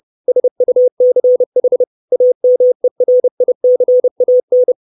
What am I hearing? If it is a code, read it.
Morse code, heard as a keyed tone, about 30 wpm, 500 Hz. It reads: SUCH AMERICAN
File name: 632 Words 30WPM